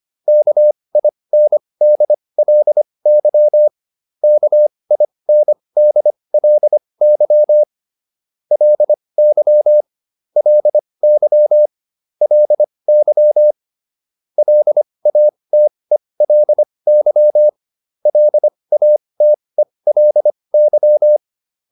Morse code, heard as a keyed tone, about 25 wpm, spaced as for 20 wpm, 600 Hz. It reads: KINDLY KINDLY LY LY LY LATELY LATELY
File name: LY6 20